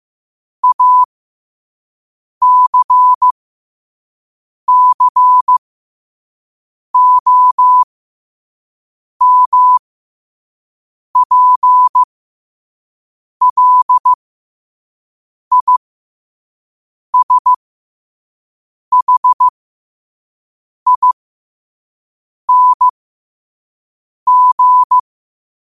Morse code, heard as a keyed tone, about 15 wpm, spaced as for 5 wpm, 1000 Hz. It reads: ACCOMPLISHING
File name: qotdfiles / 5wpm